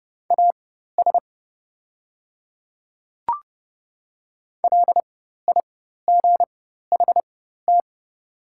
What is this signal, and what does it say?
Morse code, heard as a keyed tone, about 30 wpm, spaced as for 14 wpm, 700 Hz. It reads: AS LIGHT
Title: 4x Character Spacing - Top 500 Words - 30wpm